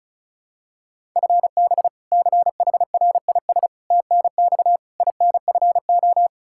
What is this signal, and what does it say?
Morse code, heard as a keyed tone, about 34 wpm, 700 Hz. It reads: FB CHRIS TNX INFO
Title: Practice QSO - Exchange 3 - More Info - 35wpm